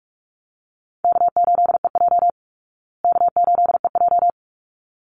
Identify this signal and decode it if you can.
Morse code, heard as a keyed tone, about 44 wpm, 700 Hz. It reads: K8EJ K8EJ
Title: Easy Call Signs Repeated 2x - 45wpm